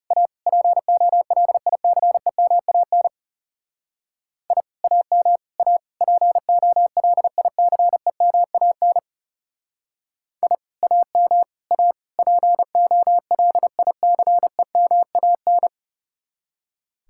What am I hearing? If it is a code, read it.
Morse code, heard as a keyed tone, about 39 wpm, 700 Hz. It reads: A POLICEMAN I AM A POLICEMAN I AM A POLICEMAN